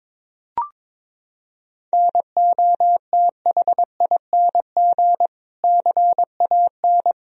Morse code, heard as a keyed tone, about 22 wpm, 700 Hz. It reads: NOTHING CAN
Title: Sentences from Top 1000 Words - Review Part 4 - 22wpm